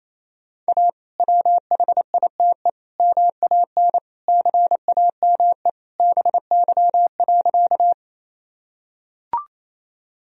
Morse code, heard as a keyed tone, about 28 wpm, 700 Hz. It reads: A WHITE MAN CAME BY.
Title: Sentences from Top 1000 Words - Review Part 1 - 28wpm